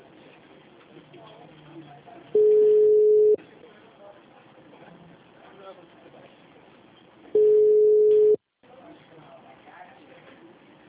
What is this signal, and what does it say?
Ringback tone (Europe, 425 Hz), with speech.